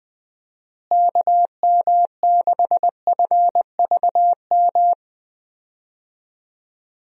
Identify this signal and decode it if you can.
Morse code, heard as a keyed tone, about 21 wpm, 700 Hz. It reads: KM6FVM
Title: USA Call Signs - 6 Characters - Repeated 2x - 20wpm